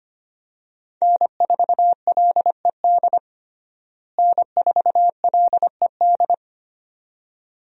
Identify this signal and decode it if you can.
Morse code, heard as a keyed tone, about 25 wpm, 700 Hz. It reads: N4LED N4LED
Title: USA Call Signs - 5 Characters - Repeated 2x - 25wpm